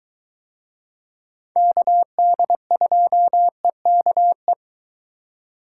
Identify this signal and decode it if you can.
Morse code, heard as a keyed tone, about 23 wpm, 700 Hz. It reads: KD2EKE